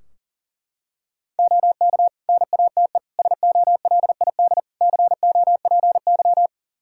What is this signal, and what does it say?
Morse code, heard as a keyed tone, about 37 wpm, 700 Hz. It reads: OK NATE SOLID COPY
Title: Practice QSO - Exchange 3 - More Info - 40wpm